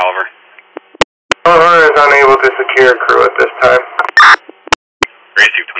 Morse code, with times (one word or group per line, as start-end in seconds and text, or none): none